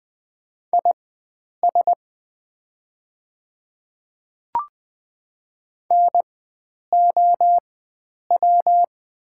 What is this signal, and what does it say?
Morse code, heard as a keyed tone, about 20 wpm, spaced as for 9 wpm, 700 Hz. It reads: IS NOW
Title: 4x Character Spacing - Top 100 Words - 20wpm